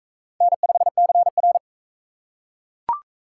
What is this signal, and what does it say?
Morse code, heard as a keyed tone, about 42 wpm, 700 Hz. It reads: N5CR